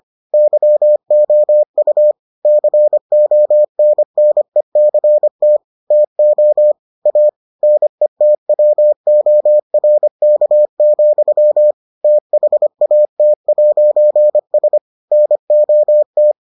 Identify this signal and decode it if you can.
Morse code, heard as a keyed tone, about 25 wpm, 600 Hz. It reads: YOU CONNECT TO A NETWORK, THAT'S NOT